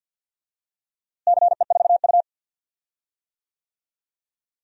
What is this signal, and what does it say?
Morse code, heard as a keyed tone, about 49 wpm, 700 Hz. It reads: CE4U
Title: DXCC-100 Call Signs - 4 Characters - Repeated 2x - 50wpm